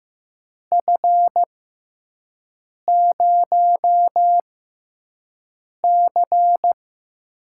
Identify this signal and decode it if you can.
Morse code, heard as a keyed tone, about 15 wpm, spaced as for 5 wpm, 700 Hz. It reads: F0C
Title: USA Call Signs - 4 Characters - 6x Character Spacing - 15wpm